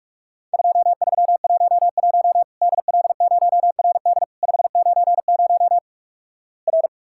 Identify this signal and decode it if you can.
Morse code, heard as a keyed tone, about 46 wpm, 700 Hz. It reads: 2311 DL0RD 590 R
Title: Worked All Europe DX Contest - QTC Exchange - 45wpm